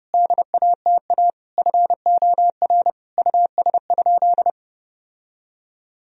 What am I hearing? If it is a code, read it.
Morse code, heard as a keyed tone, about 30 wpm, 700 Hz. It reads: DATA FOR US?
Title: Sentences from Top 1000 Words - Review Part 3 - 30wpm